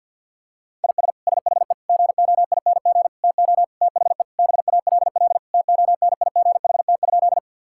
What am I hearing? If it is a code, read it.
Morse code, heard as a keyed tone, about 46 wpm, 700 Hz. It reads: IS SHE GOING TO THE BALL TONIGHT?
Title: Sentences from Top 1000 Words - Review Part 6 - 50wpm